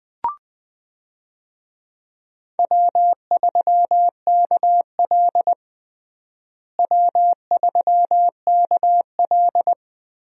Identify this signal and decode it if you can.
Morse code, heard as a keyed tone, about 20 wpm, 700 Hz. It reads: W3KL W3KL
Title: USA Call Signs - 4 Characters - Repeated 3x - 20wpm